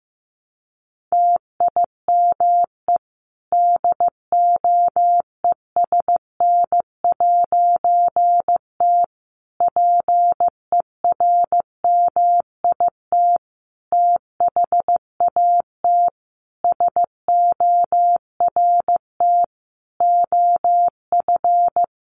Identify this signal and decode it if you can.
Morse code, heard as a keyed tone, about 15 wpm, 700 Hz. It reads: TIME DOESN'T PERMIT THAT SORT OF
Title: A Sound Of Thunder 15wpm